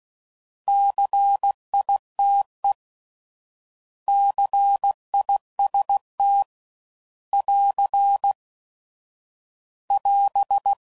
Morse code, heard as a keyed tone, about 16 wpm, 800 Hz. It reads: CITE CIST <AR> <AS>
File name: Prefixes / By First Letter / Cs